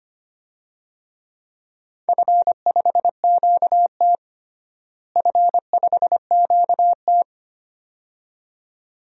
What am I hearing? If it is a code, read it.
Morse code, heard as a keyed tone, about 25 wpm, 700 Hz. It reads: F5QT F5QT
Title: DXCC-100 Call Signs - 4 Characters - Repeated 2x - 25wpm